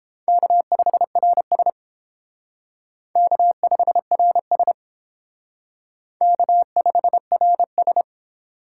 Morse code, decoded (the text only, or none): K5RS K5RS K5RS